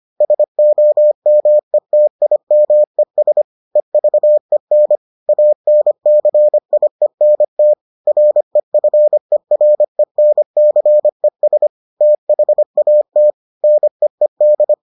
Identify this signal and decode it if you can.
Morse code, heard as a keyed tone, about 25 wpm, 600 Hz. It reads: SOMETIMES EVEN ANCIENT REFERENCES THAT NEED